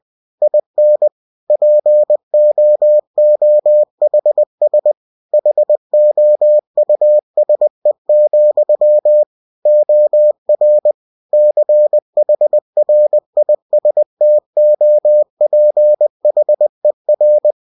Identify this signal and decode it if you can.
Morse code, heard as a keyed tone, about 20 wpm, 600 Hz. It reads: IN POOHS HOUSE, OR CHRISTOPHER
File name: cw-books / winnie-the-pooh / winnie-20wpm-page-0090